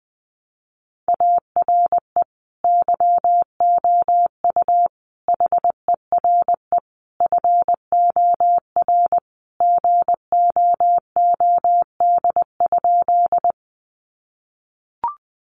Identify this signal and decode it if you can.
Morse code, heard as a keyed tone, about 20 wpm, 700 Hz. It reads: ARE YOU HERE FOR GOOD?
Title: Sentences from Top 200 Words 20wpm